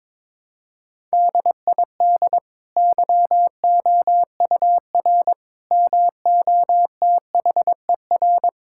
Morse code, heard as a keyed tone, about 22 wpm, 700 Hz. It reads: DID YOUR MOTHER
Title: Sentences from Top 1000 Words - Review Part 6 - 22wpm